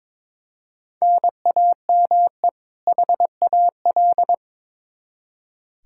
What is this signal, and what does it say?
Morse code, heard as a keyed tone, about 21 wpm, 700 Hz. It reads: NAME HAL
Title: Practice QSO - Exchange 1 - Beginning - 22wpm